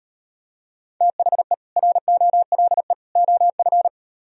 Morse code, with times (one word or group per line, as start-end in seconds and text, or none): 1.0-1.5 THE
1.8-2.9 ROLE
3.2-3.9 OF